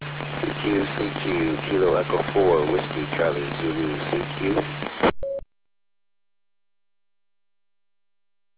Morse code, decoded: none

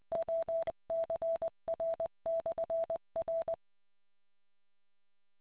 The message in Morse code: WCR/R

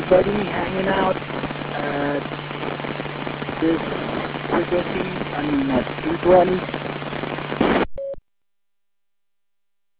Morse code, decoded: none

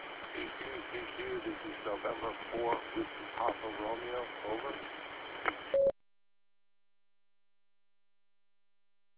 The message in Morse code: none

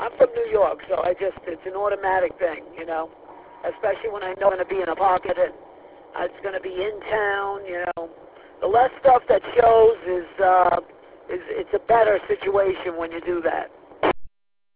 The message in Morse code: none

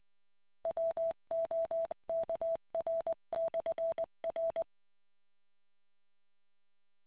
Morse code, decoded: WOKR/R

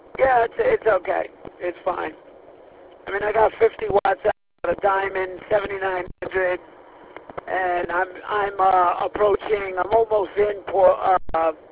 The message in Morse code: none

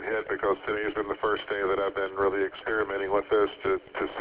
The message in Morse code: none

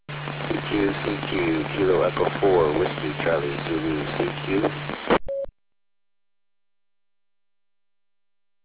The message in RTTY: none

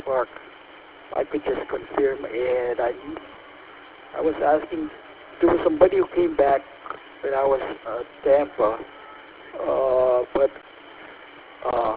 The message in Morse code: none

